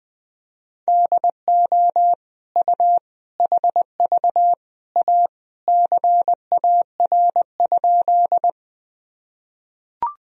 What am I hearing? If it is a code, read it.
Morse code, heard as a keyed tone, about 20 wpm, 700 Hz. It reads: DO U HV A CAR?
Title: Sentences from Top 300 Words and Abbreviations 20wpm